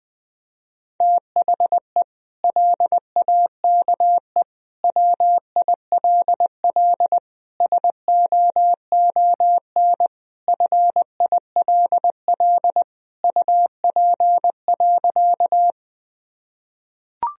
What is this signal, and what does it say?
Morse code, heard as a keyed tone, about 21 wpm, 700 Hz. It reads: THE LAKE WILL SOON FILL UP.
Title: Sentences from Top 1000 Words - Review Part 3 - 20wpm